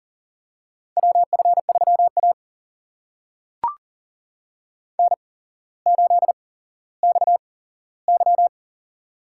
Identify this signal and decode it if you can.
Morse code, heard as a keyed tone, about 39 wpm, spaced as for 17 wpm, 700 Hz. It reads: WF3A N8XY
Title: USA Call Signs - 4 Characters - 8x Character Spacing - 40wpm